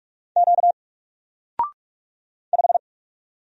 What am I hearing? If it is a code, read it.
Morse code, heard as a keyed tone, about 44 wpm, 700 Hz. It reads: Q 5
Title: Single Character (Letter:Number:Punctuation:Prosign) 45wpm